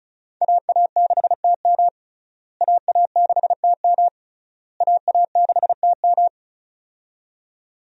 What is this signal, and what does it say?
Morse code, heard as a keyed tone, about 34 wpm, 700 Hz. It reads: AA6TM AA6TM AA6TM
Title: USA Call Signs - 5 Characters - Repeated 3x - 35wpm